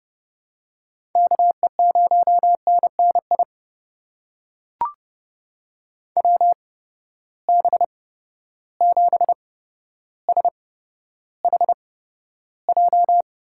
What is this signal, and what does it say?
Morse code, heard as a keyed tone, about 30 wpm, spaced as for 13 wpm, 700 Hz. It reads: KE0NNI WB7SHJ
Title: USA Call Signs - 8x Character Spacing - 30wpm